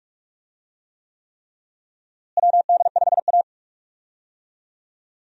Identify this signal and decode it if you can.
Morse code, heard as a keyed tone, about 46 wpm, 700 Hz. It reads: WD5A
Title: USA Call Signs - 4 Characters - Repeated 3x - 45wpm